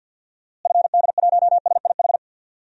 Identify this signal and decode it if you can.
Morse code, heard as a keyed tone, about 52 wpm, 700 Hz. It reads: FD1SIH
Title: DXCC-100 Call Signs - 6 Characters - Repeated 3x - 50wpm